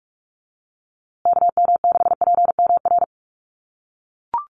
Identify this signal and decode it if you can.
Morse code, heard as a keyed tone, about 44 wpm, 700 Hz. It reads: KM6PMR
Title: USA Call Signs Repeated 2x 45wpm